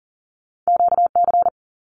Morse code, heard as a keyed tone, about 39 wpm, 700 Hz. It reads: QC
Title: Canadian Provinces and Territories 40wpm